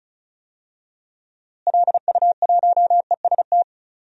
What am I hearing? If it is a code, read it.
Morse code, heard as a keyed tone, about 36 wpm, 700 Hz. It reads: LU1EST
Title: DXCC-100 Call Signs - 6 Characters - Repeated 3x - 35wpm